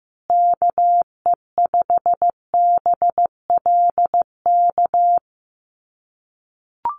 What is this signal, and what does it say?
Morse code, heard as a keyed tone, about 15 wpm, 700 Hz. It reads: KE5BLK
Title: USA Call Signs Repeated 2x 15wpm